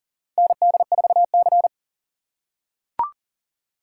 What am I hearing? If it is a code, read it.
Morse code, heard as a keyed tone, about 40 wpm, 700 Hz. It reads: ND4C
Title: USA Call Signs - 4 Characters - 4x Character Spacing - 40wpm